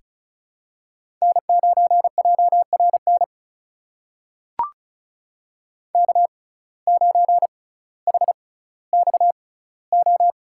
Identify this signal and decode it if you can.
Morse code, heard as a keyed tone, about 36 wpm, spaced as for 18 wpm, 700 Hz. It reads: N9JRN K9HXO
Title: USA Call Signs - 5 Characters - 6x Character Spacing - 35wpm